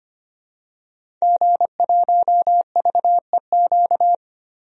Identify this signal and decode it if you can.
Morse code, heard as a keyed tone, about 25 wpm, 700 Hz. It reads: G1VEQ
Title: DXCC-100 Call Signs - 5 Characters - 25wpm